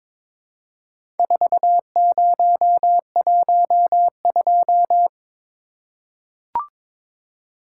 Morse code, heard as a keyed tone, about 22 wpm, 700 Hz. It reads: 4012
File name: practice / 22wpm / Four Letter-Number Combinations - 22wpm